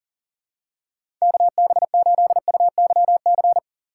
Morse code, heard as a keyed tone, about 40 wpm, 700 Hz. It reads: KB8UYC
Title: USA Call Signs - 6 Characters - 2x Character Spacing - 40wpm